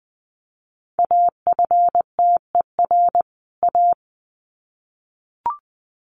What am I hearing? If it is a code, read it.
Morse code, heard as a keyed tone, about 20 wpm, 700 Hz. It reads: AFTER A
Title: Sets of 2 Words 20wpm